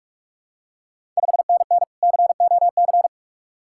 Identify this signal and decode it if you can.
Morse code, heard as a keyed tone, about 44 wpm, 690 Hz. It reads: 5NN COC
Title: Ontario QSO Party Exchange 45wpm